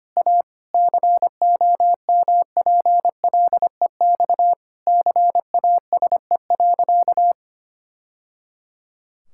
Morse code, heard as a keyed tone, about 25 wpm, 700 Hz. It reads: A COMPLEX CASE.
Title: Sentences from Top 1000 Words - Review Part 6 - 25wpm